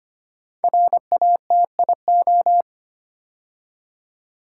Morse code, heard as a keyed tone, about 26 wpm, 700 Hz. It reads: RATIO
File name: 5 Letter Words - First 3 Letters Given - 25wpm